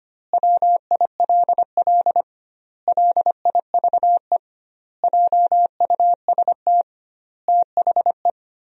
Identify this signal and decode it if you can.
Morse code, heard as a keyed tone, about 25 wpm, 700 Hz. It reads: WILL LIVE JUST THE
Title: 2x Word-Spacing - Sentences from Top 300 Words - 25wpm